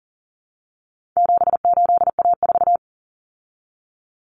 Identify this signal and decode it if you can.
Morse code, heard as a keyed tone, about 40 wpm, 700 Hz. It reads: 78A4
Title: Common Cut Numbers - 4-Digit 40wpm